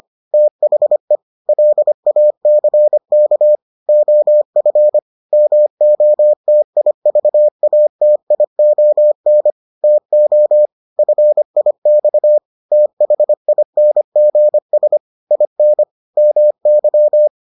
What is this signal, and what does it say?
Morse code, heard as a keyed tone, about 25 wpm, 600 Hz. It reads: THE LACK OF MOTIVATION TO FIX THINGS IN MY